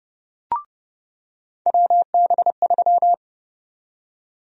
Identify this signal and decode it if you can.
Morse code, heard as a keyed tone, about 30 wpm, 700 Hz. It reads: WB3
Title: USA Call Sign Prefixes - 30wpm